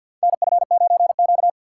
Morse code, heard as a keyed tone, about 53 wpm, 700 Hz. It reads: NF9Q
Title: USA Call Signs - 4 Characters - 3x Character Spacing - 50wpm